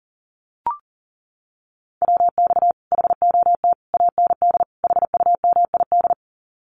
Morse code, heard as a keyed tone, about 38 wpm, 700 Hz. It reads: WX HOT AND HUMID
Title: Common QSO Phrases Encore 40wpm